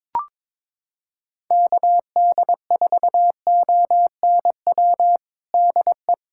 Morse code, heard as a keyed tone, about 22 wpm, 700 Hz. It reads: KD4ONW DE
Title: USA Callsign DE Callsign - 6 Characters - 22wpm